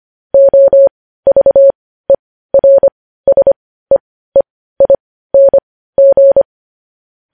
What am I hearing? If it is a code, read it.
Morse code, heard as a keyed tone, about 25 wpm, spaced as for 15 wpm, 550 Hz. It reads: OVERSEEING
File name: podcast / News headlines-1x-25x15